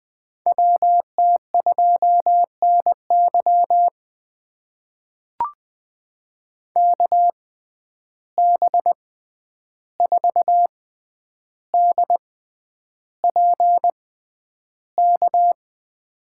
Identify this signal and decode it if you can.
Morse code, heard as a keyed tone, about 21 wpm, spaced as for 10 wpm, 700 Hz. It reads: WT2NY KB4DPK
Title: USA Call Signs - 6x Character Spacing - 20wpm